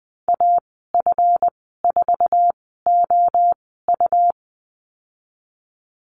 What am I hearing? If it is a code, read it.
Morse code, heard as a keyed tone, about 19 wpm, spaced as for 14 wpm, 700 Hz. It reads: AF4OU
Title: 173 Lesson 28g - Number 9 - Call Signs 20wpm